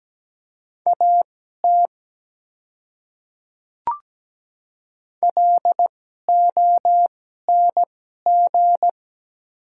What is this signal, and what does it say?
Morse code, heard as a keyed tone, about 17 wpm, spaced as for 12 wpm, 700 Hz. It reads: AT LONG
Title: 2x Character Spacing - Top 100 Words - 17wpm